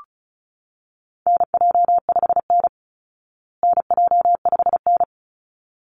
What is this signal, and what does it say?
Morse code, heard as a keyed tone, about 34 wpm, 700 Hz. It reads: NJ5N NJ5N
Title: Call Signs Repeated 3x - 35wpm